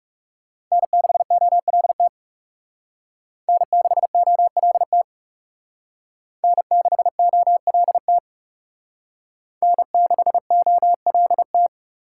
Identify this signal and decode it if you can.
Morse code, heard as a keyed tone, about 37 wpm, 700 Hz. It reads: N6OLT N6OLT N6OLT N6OLT
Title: Speed Racing - USA 1x3 Call Signs 30wpm